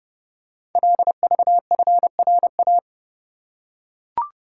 Morse code, heard as a keyed tone, about 30 wpm, 700 Hz. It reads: LVFRA
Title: Code Groups - 5 Letters - 30wpm